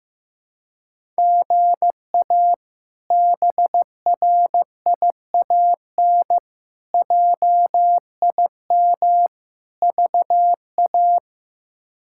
Morse code, heard as a keyed tone, about 15 wpm, 700 Hz. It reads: GA BRIAN JIM VA
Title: K1USN SST QSO Practice - 15wpm